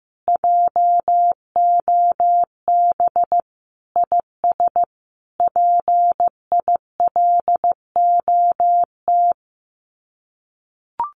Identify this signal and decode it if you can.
Morse code, heard as a keyed tone, about 15 wpm, 700 Hz. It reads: JOB IS PILOT
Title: Common QSO Phrases Encore 15wpm